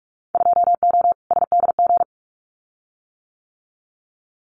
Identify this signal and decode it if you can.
Morse code, heard as a keyed tone, about 46 wpm, 700 Hz. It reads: 2O SDG
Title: Winter Field Day Exchanges 45wpm